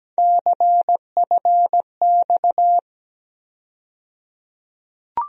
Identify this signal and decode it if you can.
Morse code, heard as a keyed tone, about 17 wpm, 700 Hz. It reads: CFX
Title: USA Call Sign Postfixes - 3 Letters 17wpm